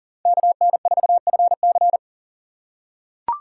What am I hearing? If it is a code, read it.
Morse code, heard as a keyed tone, about 40 wpm, 700 Hz. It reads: KN4FC
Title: USA Call Signs - 4x Character Spacing - 40wpm